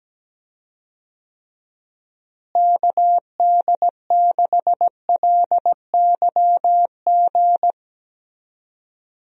Morse code, heard as a keyed tone, about 17 wpm, 700 Hz. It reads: KD6LYG
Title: USA Call Signs - 6 Characters - Repeated 2x - 17wpm